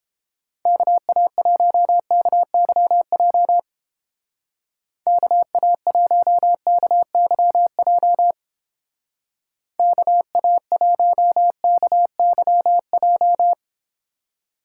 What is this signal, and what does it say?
Morse code, heard as a keyed tone, about 31 wpm, 700 Hz. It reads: KA1KYJ KA1KYJ KA1KYJ